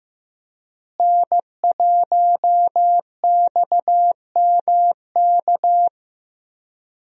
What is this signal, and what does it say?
Morse code, heard as a keyed tone, about 15 wpm, 700 Hz. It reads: N1XMK
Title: USA Call Signs - 5 Characters - Repeated 2x - 15wpm